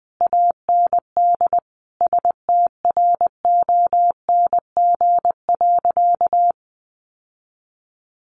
Morse code, heard as a keyed tone, about 19 wpm, 700 Hz. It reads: AND STRONG.
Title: Sentences from Top 1400 Words 20wpm